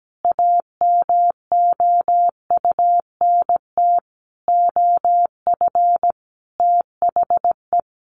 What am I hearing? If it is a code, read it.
Morse code, heard as a keyed tone, about 17 wpm, 700 Hz. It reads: AMOUNT OF THE
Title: Sentences from Top 1500 Words 17wpm